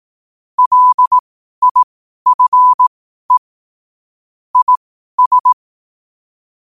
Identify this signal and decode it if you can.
Morse code, heard as a keyed tone, about 18 wpm, spaced as for 12 wpm, 1000 Hz. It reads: LIFE IS
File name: qotdfiles / 12wpm